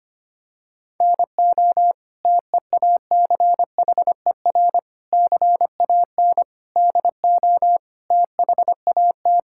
Morse code, heard as a keyed tone, about 25 wpm, 700 Hz. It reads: NO TEACHER CAN DO THAT
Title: Sentences from Top 1000 Words - Review Part 6 - 25wpm